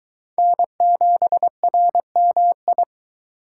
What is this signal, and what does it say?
Morse code, heard as a keyed tone, about 21 wpm, 700 Hz. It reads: N7RMI